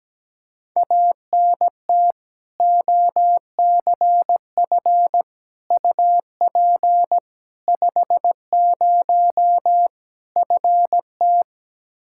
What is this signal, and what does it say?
Morse code, heard as a keyed tone, about 17 wpm, 700 Hz. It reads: ANT OCF UP 50 FT
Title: Practice QSO - Exchange 2 - Info - 17wpm